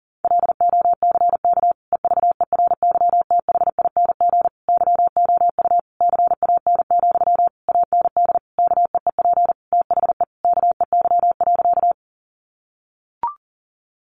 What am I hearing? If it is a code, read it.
Morse code, heard as a keyed tone, about 40 wpm, 700 Hz. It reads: LOCK EVERYTHING YOU CAN, AND KEEP THE KEY.
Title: Sentences from Top 1300 Words 40wpm